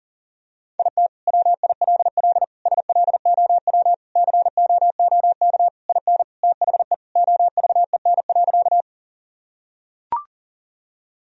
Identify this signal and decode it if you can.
Morse code, heard as a keyed tone, about 40 wpm, 700 Hz. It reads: IT WILL SLOW COOK IN THE OVEN.
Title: Sentences from Top 1000 Words - Review Part 4 - 40wpm